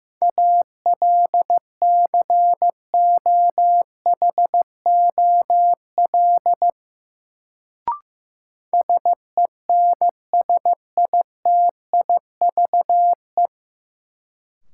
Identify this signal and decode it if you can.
Morse code, heard as a keyed tone, about 15 wpm, 700 Hz. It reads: ALCOHOL SENSITIVE
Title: Sentences from Top 1400 Words - New Word Drill - 15wpm